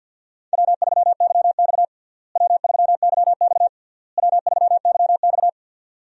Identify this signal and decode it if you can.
Morse code, heard as a keyed tone, about 52 wpm, 700 Hz. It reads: W3YX W3YX W3YX
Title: USA Call Signs - 4 Characters - Repeated 3x - 50wpm